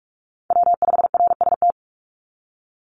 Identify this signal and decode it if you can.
Morse code, heard as a keyed tone, about 46 wpm, 700 Hz. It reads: W5RST
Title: NA SOTA Chaser Callsigns 45wpm